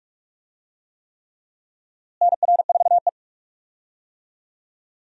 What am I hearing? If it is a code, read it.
Morse code, heard as a keyed tone, about 46 wpm, 700 Hz. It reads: NR4E
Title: USA Call Signs - 4 Characters - Repeated 3x - 45wpm